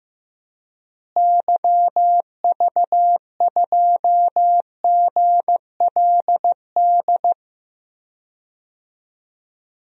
Morse code, heard as a keyed tone, about 15 wpm, 700 Hz. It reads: YV2GLD
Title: DXCC-100 Call Signs - 6 Characters - Repeated 2x - 15wpm